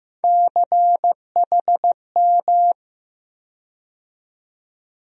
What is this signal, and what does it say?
Morse code, heard as a keyed tone, about 15 wpm, 700 Hz. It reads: CHM
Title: USA Call Sign Postfixes - 3 Letters 15wpm